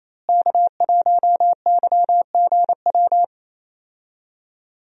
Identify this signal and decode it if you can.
Morse code, heard as a keyed tone, about 28 wpm, 700 Hz. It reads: K1YGW
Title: USA Call Signs - 5 Characters - Repeated 3x - 28wpm